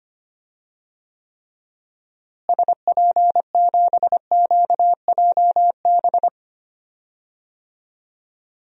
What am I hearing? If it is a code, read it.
Morse code, heard as a keyed tone, about 25 wpm, 700 Hz. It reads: SP7QJB
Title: DXCC-100 Call Signs - 6 Characters - Repeated 3x - 25wpm